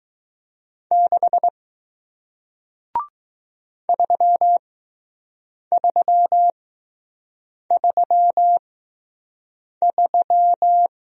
Morse code, heard as a keyed tone, about 20 wpm, spaced as for 6 wpm, 700 Hz. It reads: 6 3333